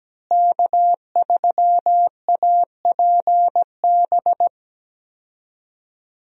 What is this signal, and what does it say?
Morse code, heard as a keyed tone, about 17 wpm, 700 Hz. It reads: K3APB
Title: USA Call Signs - 5 Characters - Repeated 2x - 17wpm